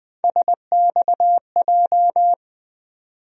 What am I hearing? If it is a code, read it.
Morse code, heard as a keyed tone, about 20 wpm, 700 Hz. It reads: SXJ
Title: USA Call Sign Postfixes - 3 Letters 20wpm